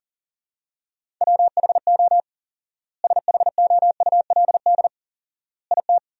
Morse code, heard as a keyed tone, about 40 wpm, 700 Hz. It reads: WHO SHOULD IT
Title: 4x Word-Spacing - Sentences from Top 300 Words - 40wpm